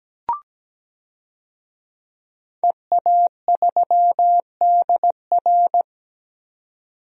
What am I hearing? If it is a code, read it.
Morse code, heard as a keyed tone, about 17 wpm, 700 Hz. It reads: EA3DR
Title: DXCC-100 Call Signs - 5 Characters - 17wpm